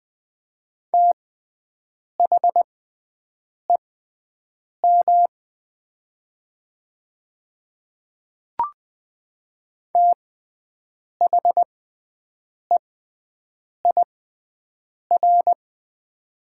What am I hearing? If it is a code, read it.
Morse code, heard as a keyed tone, about 20 wpm, spaced as for 7 wpm, 700 Hz. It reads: THEM THEIR
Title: 6x Character Spacing - Top 300 Words - 20wpm